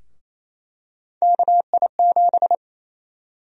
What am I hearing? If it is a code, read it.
Morse code, heard as a keyed tone, about 28 wpm, 700 Hz. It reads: KI7
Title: USA Call Sign Prefixes - Repeated 2x - 28wpm